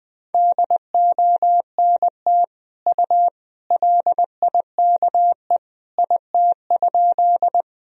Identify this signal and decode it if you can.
Morse code, heard as a keyed tone, about 20 wpm, 700 Hz. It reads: DONT U LIKE IT?
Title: Sentences from Top 300 Words and Abbreviations 20wpm